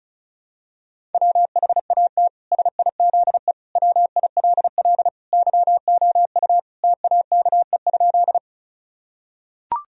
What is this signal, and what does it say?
Morse code, heard as a keyed tone, about 36 wpm, 700 Hz. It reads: WHAT SIZE WILL YOU TAKE?
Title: Sentences from Top 1000 Words - Review Part 2 - 35wpm